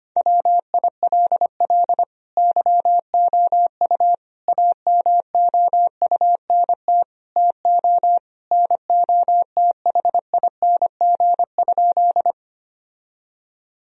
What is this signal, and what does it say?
Morse code, heard as a keyed tone, about 25 wpm, 700 Hz. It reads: WILL YOU AMOUNT TO NOTHING?
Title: Sentences from Top 1000 Words - Review Part 4 - 25wpm